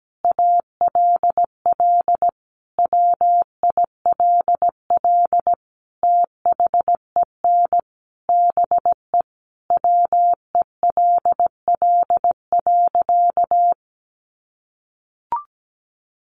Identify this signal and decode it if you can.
Morse code, heard as a keyed tone, about 17 wpm, 700 Hz. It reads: ALL WILL THEN BE WELL.
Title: Sentences from Top 1500 Words 17wpm